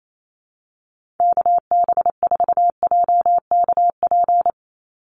Morse code, heard as a keyed tone, about 28 wpm, 700 Hz. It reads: KB4JKP
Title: USA Call Signs 28wpm